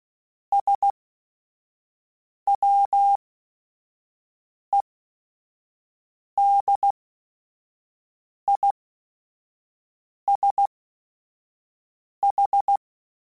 Morse code, heard as a keyed tone, about 16 wpm, spaced as for 5 wpm, 800 Hz. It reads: SWEDISH